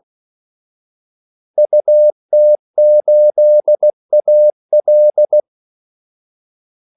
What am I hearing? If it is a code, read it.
Morse code, heard as a keyed tone, about 16 wpm, 600 Hz. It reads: UT8AL
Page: Kaldesignaler 16 wpm